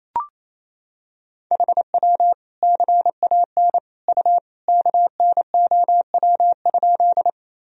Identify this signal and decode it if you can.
Morse code, heard as a keyed tone, about 28 wpm, 700 Hz. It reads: HW CAN U KNOW?
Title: Sentences from Top 100 Words and Abbreviations 28wpm